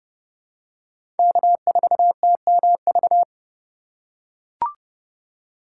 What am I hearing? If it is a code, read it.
Morse code, heard as a keyed tone, about 30 wpm, 700 Hz. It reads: K4TMV
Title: USA Call Signs - 5 Characters - 4x Character Spacing - 30wpm